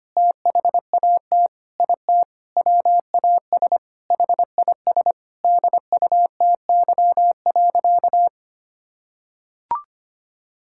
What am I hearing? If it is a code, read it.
Morse code, heard as a keyed tone, about 25 wpm, 700 Hz. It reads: THAT IT WAS HIS DUTY.
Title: Sentences from Top 1600 Words 25wpm